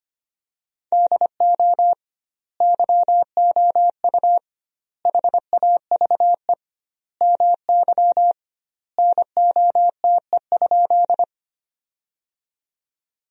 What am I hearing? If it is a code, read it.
Morse code, heard as a keyed tone, about 25 wpm, 700 Hz. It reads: DO YOU HAVE MY NOTE?
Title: 2x Word-Spacing - Sentences from Top 500 Words - 25wpm